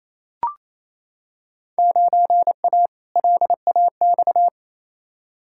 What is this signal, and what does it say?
Morse code, heard as a keyed tone, about 28 wpm, 700 Hz. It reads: 9A LAX
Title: Field Day Exchanges Encore - 28wpm